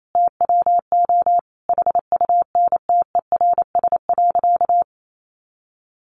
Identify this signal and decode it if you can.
Morse code, heard as a keyed tone, about 28 wpm, 700 Hz. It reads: TWO HUNTERS.
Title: A Sound Of Thunder 28wpm